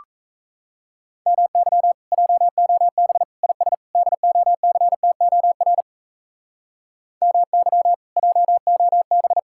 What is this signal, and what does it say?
Morse code, heard as a keyed tone, about 36 wpm, 700 Hz. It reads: MY JOB IS DOCTOR MY JOB